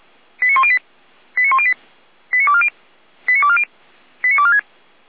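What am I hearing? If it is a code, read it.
Five-tone selective call (ZVEI), heard as quick runs of five tones, tones of about 80 ms. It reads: 89108 89108 89240 89240 89246
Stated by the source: ZVEI five-tone selective call received by SDR, Feuerwehr Orlamünde.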